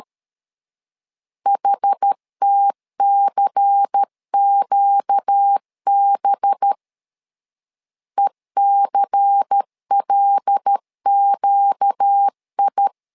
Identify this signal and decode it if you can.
Morse code, heard as a keyed tone, about 13 wpm, 800 Hz. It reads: HTCQB ECLQI
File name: LEZIONI / 5 LBVC / 5 LBVC 12wpm 1955